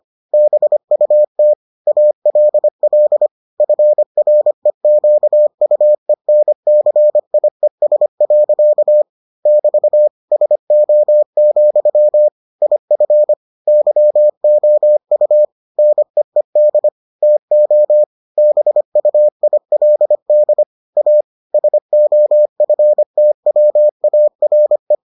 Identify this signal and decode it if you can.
Morse code, heard as a keyed tone, about 25 wpm, 600 Hz. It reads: BUT ALL FREQUENCIES. <BT> SO, IF YOU NEED TO BUILD A SOFTWARE